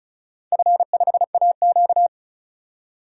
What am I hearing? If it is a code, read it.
Morse code, heard as a keyed tone, about 34 wpm, 700 Hz. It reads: F5AQ
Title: DXCC-100 Call Signs - 4 Characters - 35wpm